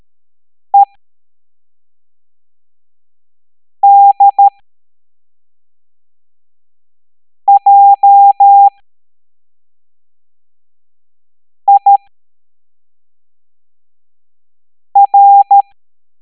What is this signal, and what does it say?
Morse code, heard as a keyed tone, about 13 wpm, spaced as for 3 wpm, 800 Hz. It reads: EDJIR